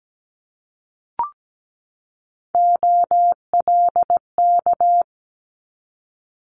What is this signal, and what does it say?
Morse code, heard as a keyed tone, about 17 wpm, 700 Hz. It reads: OLK
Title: Multi-letter Phonemes 17wpm